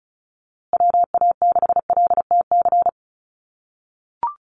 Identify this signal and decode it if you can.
Morse code, heard as a keyed tone, about 34 wpm, 700 Hz. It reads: WA6LTC
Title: USA Call Signs Repeated 3x 35wpm